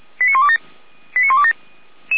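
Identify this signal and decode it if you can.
Five-tone selective call (ZVEI), heard as quick runs of five tones, tones of about 80 ms. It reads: 89127 89127
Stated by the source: SDR-received ZVEI five-tone selective call, code 89353.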